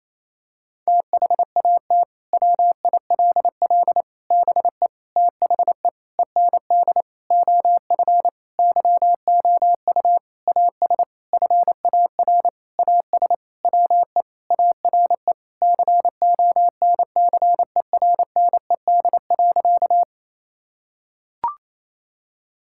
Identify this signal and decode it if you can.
Morse code, heard as a keyed tone, about 28 wpm, 700 Hz. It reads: THAT WILL BE THE END OF YOU AS FAR AS WE ARE CONCERNED.
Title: Sentences from Top 1000 Words - Review Part 6 - 28wpm